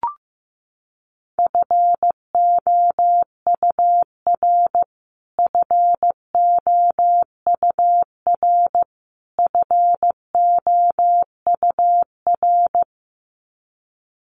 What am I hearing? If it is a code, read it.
Morse code, heard as a keyed tone, about 14 wpm, 700 Hz. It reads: FOUR FOUR FOUR
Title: Top 500 Words Repeated 15wpm